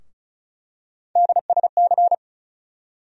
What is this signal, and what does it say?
Morse code, heard as a keyed tone, about 36 wpm, 700 Hz. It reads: DSC
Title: USA Call Sign Postfixes - 3 Letters 35wpm